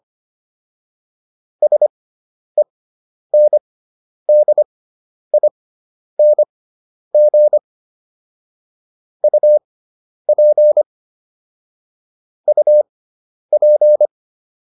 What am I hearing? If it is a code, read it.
Morse code, heard as a keyed tone, about 25 wpm, spaced as for 10 wpm, 600 Hz. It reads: SENDING UP UP